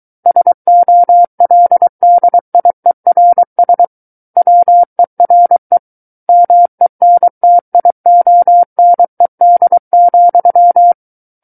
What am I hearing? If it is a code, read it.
Morse code, heard as a keyed tone, about 23 wpm, 700 Hz. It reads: SOLDIERS WERE MENTIONED,
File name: Book One / 14-Chapter 14 - 23 WPM